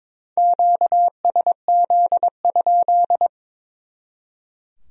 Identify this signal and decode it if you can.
Morse code, heard as a keyed tone, about 25 wpm, 700 Hz. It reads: QSZ?
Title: Speed-Racing Q-Codes - 22wpm